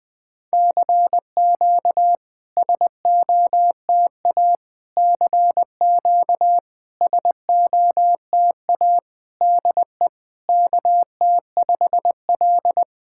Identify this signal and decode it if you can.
Morse code, heard as a keyed tone, about 23 wpm, 700 Hz. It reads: CQ SOTA CQ SOTA DE KT5L
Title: SOTA QSO Practice 20wpm